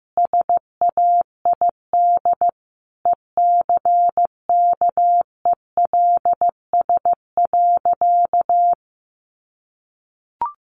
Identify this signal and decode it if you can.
Morse code, heard as a keyed tone, about 15 wpm, 700 Hz. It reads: SAID ECKELS.
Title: A Sound Of Thunder 15wpm